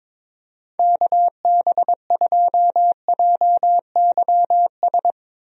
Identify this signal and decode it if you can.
Morse code, heard as a keyed tone, about 22 wpm, 700 Hz. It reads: KB2JYS
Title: USA Call Signs - 6 Characters - 22wpm